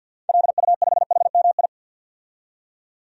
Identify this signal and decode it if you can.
Morse code, heard as a keyed tone, about 49 wpm, 700 Hz. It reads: LU5HMI
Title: DXCC-100 Call Signs - 6 Characters - 50wpm